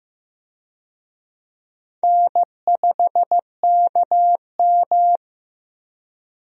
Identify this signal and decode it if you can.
Morse code, heard as a keyed tone, about 15 wpm, 700 Hz. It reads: N5KM
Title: USA Call Signs - 4 Characters - Repeated 3x - 15wpm